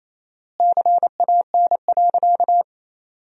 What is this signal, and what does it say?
Morse code, heard as a keyed tone, about 28 wpm, 700 Hz. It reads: CAN.
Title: 4x Word-Spacing - Sentences from Top 500 Words - 28wpm